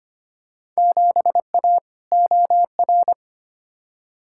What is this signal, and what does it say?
Morse code, heard as a keyed tone, about 25 wpm, 700 Hz. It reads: 7A OR
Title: Field Day Exchanges Encore - 25wpm